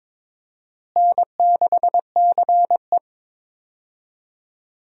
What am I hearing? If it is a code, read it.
Morse code, heard as a keyed tone, about 22 wpm, 700 Hz. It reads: N6CE
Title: USA Call Signs - 4 Characters - Repeated 3x - 22wpm